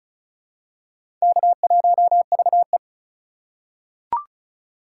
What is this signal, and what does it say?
Morse code, heard as a keyed tone, about 36 wpm, 700 Hz. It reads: K1VE
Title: USA Call Signs - 4 Characters - 2x Character Spacing - 35wpm